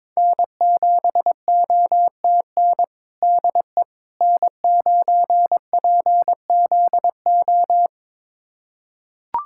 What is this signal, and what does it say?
Morse code, heard as a keyed tone, about 22 wpm, 700 Hz. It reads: N7OTN DE N9PZO
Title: USA Callsign DE Callsign - 5 Characters - 22wpm